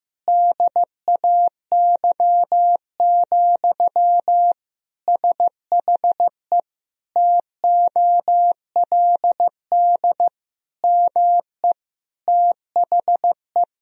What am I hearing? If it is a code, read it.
Morse code, heard as a keyed tone, about 15 wpm, 700 Hz. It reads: DAY, SHE TOLD ME THE
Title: Sentences from Top 1000 Words - Review Part 4 - 15wpm